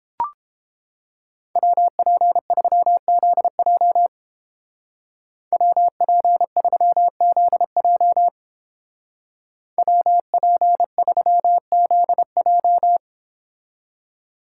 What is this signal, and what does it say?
Morse code, heard as a keyed tone, about 30 wpm, 700 Hz. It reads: WP3ZJ WP3ZJ WP3ZJ